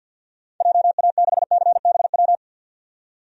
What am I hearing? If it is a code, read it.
Morse code, heard as a keyed tone, about 52 wpm, 700 Hz. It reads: JA6CBW
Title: DXCC-100 Call Signs - 6 Characters - 50wpm